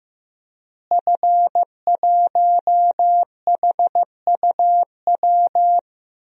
Morse code, heard as a keyed tone, about 15 wpm, 700 Hz. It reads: F1HUW
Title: DXCC-100 Call Signs - 5 Characters - Repeated 3x - 15wpm